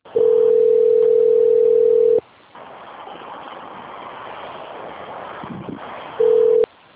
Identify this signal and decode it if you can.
Ringback tone (North America), 440 + 480 Hz.
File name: out- - - - - .62914